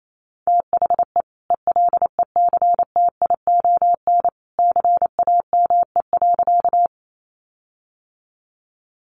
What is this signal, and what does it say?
Morse code, heard as a keyed tone, about 28 wpm, 700 Hz. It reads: THE ELECTION CAME.
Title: Sentences from Top 1500 Words 28wpm